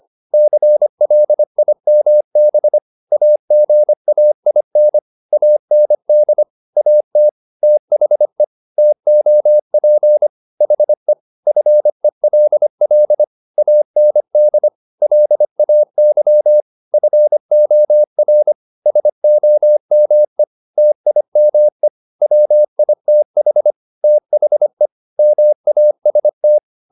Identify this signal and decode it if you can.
Morse code, heard as a keyed tone, about 25 wpm, 600 Hz. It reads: CLIMB AGAIN AND AT THE TOP HE FELL AND LAY FOR SOME TIME WITH THE MAST